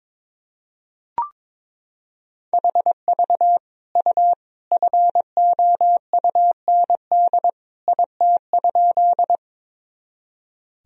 Morse code, heard as a keyed tone, about 22 wpm, 700 Hz. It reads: HV U FOUND IT?
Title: Sentences from Top 300 Words and Abbreviations 22wpm